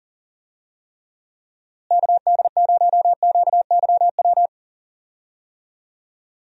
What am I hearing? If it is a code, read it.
Morse code, heard as a keyed tone, about 38 wpm, 700 Hz. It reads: KD0QYW
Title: USA Call Signs - 6 Characters - Repeated 2x - 40wpm